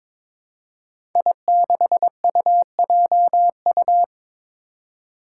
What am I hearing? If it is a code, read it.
Morse code, heard as a keyed tone, about 22 wpm, 700 Hz. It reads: I6UJU
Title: Five Challenging Letter-Number Combinations 22wpm